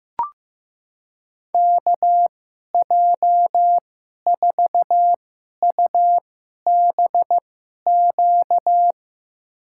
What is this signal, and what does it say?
Morse code, heard as a keyed tone, about 15 wpm, spaced as for 11 wpm, 700 Hz. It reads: KJ4UBQ
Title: USA Call Signs - 6 Characters - 2x Character Spacing - 15wpm